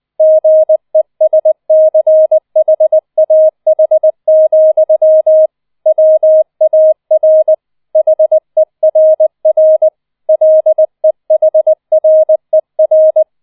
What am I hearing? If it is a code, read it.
Morse code, heard as a keyed tone, about 20 wpm, 630 Hz. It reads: GESCHAH, WAR HERR LEHRER
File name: 004 Vierter Streich 20wpm